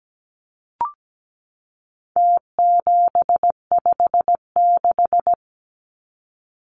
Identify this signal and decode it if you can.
Morse code, heard as a keyed tone, about 17 wpm, 700 Hz. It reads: T756
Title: Numbers - Contest Style - 4-Digit 17wpm